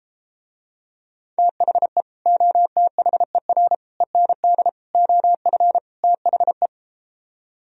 Morse code, heard as a keyed tone, about 33 wpm, 700 Hz. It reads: THE OTHER END OF THE